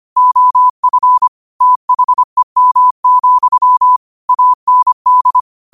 Morse code, heard as a keyed tone, about 25 wpm, 1000 Hz. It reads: OF THEM, AND
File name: qotdfiles / 25wpm